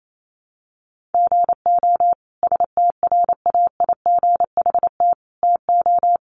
Morse code, heard as a keyed tone, about 28 wpm, 700 Hz. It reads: GO STRAIGHT TO
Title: Sentences from Top 1000 Words 28wpm